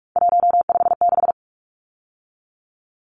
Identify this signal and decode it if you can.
Morse code, heard as a keyed tone, about 44 wpm, 700 Hz. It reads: 156
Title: Numbers - 3-Digit 45wpm